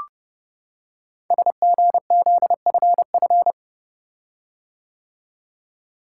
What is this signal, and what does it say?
Morse code, heard as a keyed tone, about 30 wpm, 690 Hz. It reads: SGZFF
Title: Code Groups - 5 Letters - 30wpm